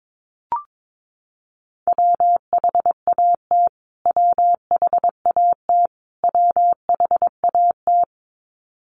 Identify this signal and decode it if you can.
Morse code, heard as a keyed tone, about 22 wpm, 700 Hz. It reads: WHAT WHAT WHAT
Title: Top 300 Words Repeated 22wpm